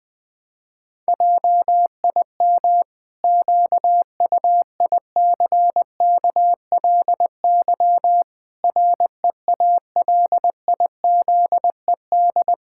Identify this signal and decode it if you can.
Morse code, heard as a keyed tone, about 21 wpm, 700 Hz. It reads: JIM QUICKLY REALIZED
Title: Pangram Challenge v2 - 20wpm